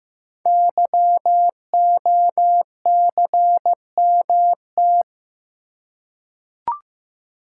Morse code, heard as a keyed tone, about 15 wpm, 700 Hz. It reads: YOCMT
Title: Code Groups - 5 Letters - 15wpm